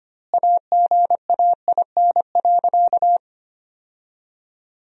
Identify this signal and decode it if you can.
Morse code, heard as a keyed tone, about 25 wpm, 700 Hz. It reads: AGAIN.
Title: 2x Word-Spacing - Sentences from Top 300 Words - 25wpm